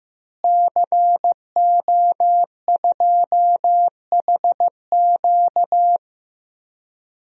Morse code, heard as a keyed tone, about 16 wpm, 700 Hz. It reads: CO2HQ
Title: DXCC-100 Call Signs - 5 Characters - Repeated 2x - 15wpm